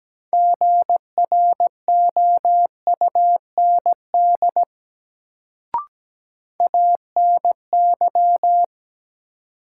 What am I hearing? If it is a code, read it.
Morse code, heard as a keyed tone, about 17 wpm, 700 Hz. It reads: GROUND ANY
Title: Sets of 3 Words - New Word Drill - 17wpm